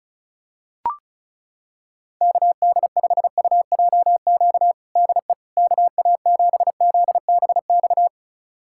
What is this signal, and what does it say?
Morse code, heard as a keyed tone, about 36 wpm, 700 Hz. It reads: KD5UJQ DE KA7ZBX
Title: USA Callsign DE Callsign - 6 Characters - 35wpm